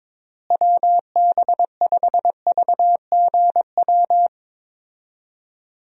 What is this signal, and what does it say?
Morse code, heard as a keyed tone, about 22 wpm, 700 Hz. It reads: WB5VGW
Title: USA Call Signs - 6 Characters - Repeated 3x - 22wpm